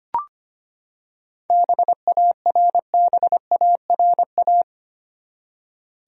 Morse code, heard as a keyed tone, about 25 wpm, 700 Hz. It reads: BARBARA
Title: USA Common Names - 25wpm